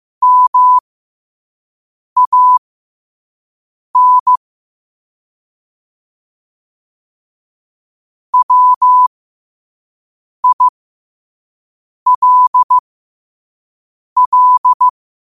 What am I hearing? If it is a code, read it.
Morse code, heard as a keyed tone, about 15 wpm, spaced as for 5 wpm, 1000 Hz. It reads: MAN WILL
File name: qotdfiles / 5wpm